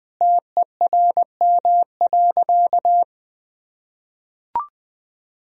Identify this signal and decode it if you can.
Morse code, heard as a keyed tone, about 21 wpm, 700 Hz. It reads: TERM.
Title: Sentences from Top 1000 Words - Review Part 4 - 20wpm